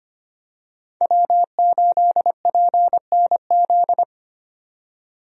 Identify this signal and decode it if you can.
Morse code, heard as a keyed tone, about 25 wpm, 700 Hz. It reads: W8PNZ
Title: USA Call Signs - 5 Characters - 25wpm